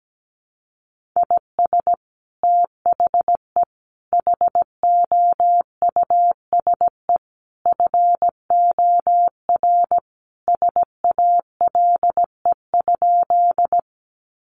Morse code, heard as a keyed tone, about 17 wpm, 700 Hz. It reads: IS THE HOUSE FOR SALE?
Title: Sentences from Top 1400 Words 17wpm